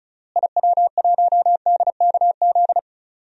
Morse code, heard as a keyed tone, about 34 wpm, 700 Hz. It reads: IW1DKZ
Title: DXCC-100 Call Signs - 6 Characters - Repeated 3x - 35wpm